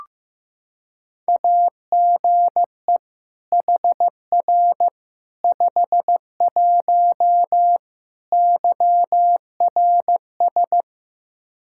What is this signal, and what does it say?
Morse code, heard as a keyed tone, about 16 wpm, 700 Hz. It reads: AGE HR 51 YRS
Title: Practice QSO - Exchange 3 - More Info - 15wpm